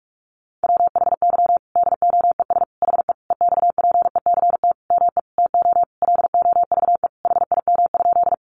Morse code, heard as a keyed tone, about 44 wpm, 700 Hz. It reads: WHY DOES HE EXPECT ME TO LOVE HIM?
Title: Sentences from Top 1100 Words 45wpm